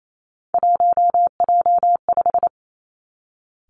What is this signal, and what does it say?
Morse code, heard as a keyed tone, about 28 wpm, 690 Hz. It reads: 1J5
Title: Three Challenging Letter-Number Combinations 28wpm